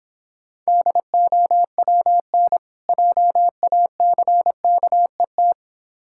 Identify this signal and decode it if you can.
Morse code, heard as a keyed tone, about 26 wpm, 700 Hz. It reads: DOWN JACKET